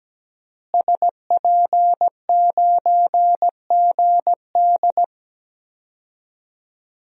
Morse code, heard as a keyed tone, about 17 wpm, 700 Hz. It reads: SP9GD
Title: DXCC-100 Call Signs - 5 Characters - Repeated 3x - 17wpm